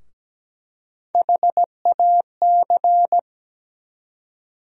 Morse code, heard as a keyed tone, about 17 wpm, 700 Hz. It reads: HAC
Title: USA Call Sign Postfixes - 3 Letters 17wpm